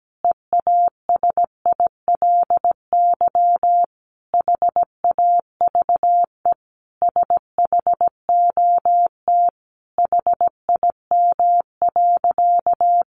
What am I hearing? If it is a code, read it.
Morse code, heard as a keyed tone, about 17 wpm, 700 Hz. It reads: EASILY HAVE SHOT HIM.
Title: Sentences from Top 1100 Words 17wpm